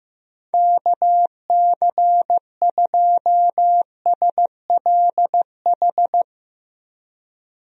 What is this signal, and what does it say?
Morse code, heard as a keyed tone, about 15 wpm, 700 Hz. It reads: KC2SLH
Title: USA Call Signs - 6 Characters - 15wpm